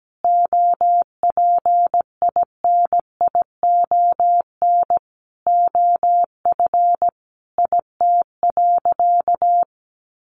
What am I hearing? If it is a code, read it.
Morse code, heard as a keyed tone, about 17 wpm, 700 Hz. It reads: OPINION OF IT.
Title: Sentences from Top 1200 Words 17wpm